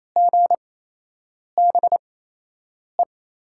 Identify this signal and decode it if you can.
Morse code, heard as a keyed tone, about 27 wpm, spaced as for 8 wpm, 700 Hz. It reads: GBE
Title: USA Call Signs - 6 Characters - 8x Character Spacing - 28wpm